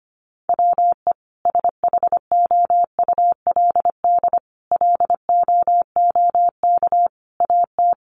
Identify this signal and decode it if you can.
Morse code, heard as a keyed tone, about 25 wpm, 700 Hz. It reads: WE SHOULD LOOK AT
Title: Sentences from Top 300 Words 25wpm